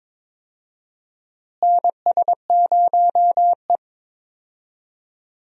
Morse code, heard as a keyed tone, about 22 wpm, 700 Hz. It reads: NS0E
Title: USA Call Signs - 4 Characters - Repeated 2x - 22wpm